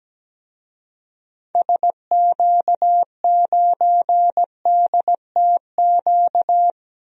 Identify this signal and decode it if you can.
Morse code, heard as a keyed tone, about 17 wpm, 700 Hz. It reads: SQ9DTQ
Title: DXCC-100 Call Signs - 6 Characters - 17wpm